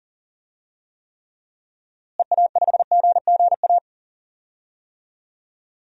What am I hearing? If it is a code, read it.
Morse code, heard as a keyed tone, about 40 wpm, 700 Hz. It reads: EA5GGA
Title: DXCC-100 Call Signs - 6 Characters - Repeated 3x - 40wpm